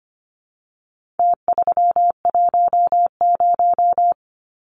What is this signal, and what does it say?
Morse code, heard as a keyed tone, about 25 wpm, 700 Hz. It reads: T310
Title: Numbers - Contest Style - 4-Digit 25wpm